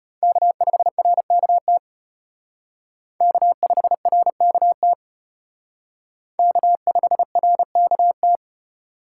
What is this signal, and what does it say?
Morse code, heard as a keyed tone, about 32 wpm, 700 Hz. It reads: K5RKT K5RKT K5RKT